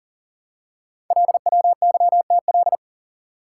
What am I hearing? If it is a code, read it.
Morse code, heard as a keyed tone, about 38 wpm, 700 Hz. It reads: LWYTL
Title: Code Groups - 5 Letters - 40wpm